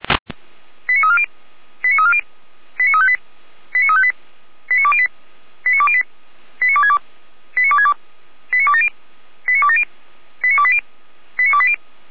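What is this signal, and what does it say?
Five-tone selective call (ZVEI), heard as quick runs of five tones, tones of about 70 ms. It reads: 89340 89340 89358 89358 89208 89208 89262 89262 89270 89270 89280 89280
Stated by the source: ZVEI five-tone call received by SDR, from feuerwehr-orlamuende.org fire brigade audio server.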